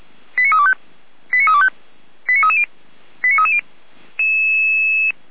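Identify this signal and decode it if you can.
Five-tone selective call (ZVEI), heard as quick runs of five tones, tones of about 70 ms. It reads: 89325 89325 89330 89330
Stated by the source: Received on an SDR.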